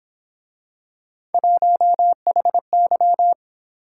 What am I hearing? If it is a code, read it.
Morse code, heard as a keyed tone, about 26 wpm, 700 Hz. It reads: 1HY
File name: Speed-Racing - Three Letter-Number Combinations - 22wpm